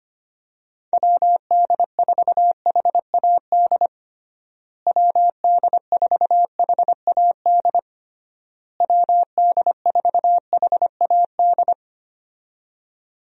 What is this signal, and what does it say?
Morse code, heard as a keyed tone, about 25 wpm, 700 Hz. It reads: WD4HAD WD4HAD WD4HAD
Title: USA Call Signs - 6 Characters - Repeated 3x - 25wpm